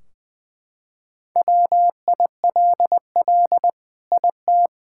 Morse code, heard as a keyed tone, about 20 wpm, 700 Hz. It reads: WILL IT
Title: Sentences from Top 1600 Words 20wpm